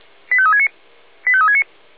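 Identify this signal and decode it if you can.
Five-tone selective call (ZVEI), heard as quick runs of five tones, tones of about 70 ms. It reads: 85369 85369
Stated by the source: Received on SDR, ZVEI five-tone call.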